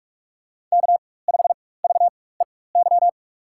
Morse code, heard as a keyed tone, about 48 wpm, spaced as for 21 wpm, 700 Hz. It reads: K5VEY
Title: USA Call Signs - 5 Characters - 4x Character Spacing - 45wpm